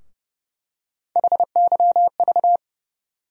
Morse code, heard as a keyed tone, about 30 wpm, 700 Hz. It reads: HYV
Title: USA Call Sign Postfixes - 3 Letters 30wpm